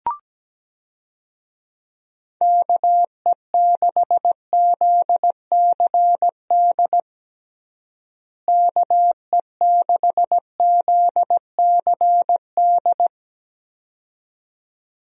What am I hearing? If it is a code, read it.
Morse code, heard as a keyed tone, about 17 wpm, 700 Hz. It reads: KE6ZCD KE6ZCD
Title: USA Call Signs - 6 Characters - Repeated 2x - 17wpm